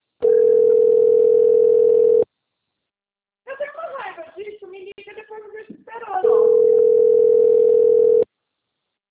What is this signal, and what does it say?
Ringback tone (North America), 440 + 480 Hz, with speech.
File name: rg-1600-unknown- - - .10968